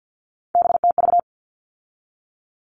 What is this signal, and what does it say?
Morse code, heard as a keyed tone, about 50 wpm, 700 Hz. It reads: BTV
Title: All Cut Numbers - 3-Digit 50wpm